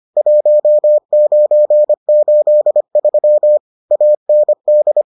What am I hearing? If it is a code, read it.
Morse code, heard as a keyed tone, about 25 wpm, 600 Hz. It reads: 1983 AND